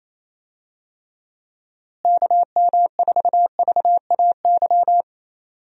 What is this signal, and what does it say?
Morse code, heard as a keyed tone, about 27 wpm, 700 Hz. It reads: KM4VAY
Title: USA Call Signs - 6 Characters - Repeated 2x - 28wpm